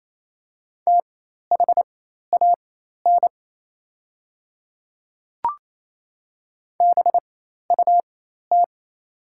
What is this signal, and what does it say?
Morse code, heard as a keyed tone, about 28 wpm, spaced as for 13 wpm, 700 Hz. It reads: THAN BUT
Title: 4x Character Spacing - Top 100 Words - 28wpm